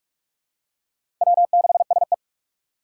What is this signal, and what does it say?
Morse code, heard as a keyed tone, about 41 wpm, 700 Hz. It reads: W6SE
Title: USA Call Signs - 6x Character Spacing - 45wpm